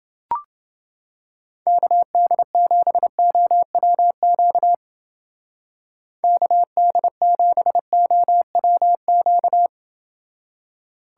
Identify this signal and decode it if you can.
Morse code, heard as a keyed tone, about 28 wpm, 700 Hz. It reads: KD7OWQ KD7OWQ